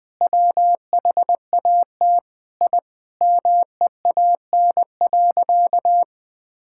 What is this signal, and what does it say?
Morse code, heard as a keyed tone, about 21 wpm, 700 Hz. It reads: WHAT I MEAN.
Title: Sentences with 4-Word Phrases - 20wpm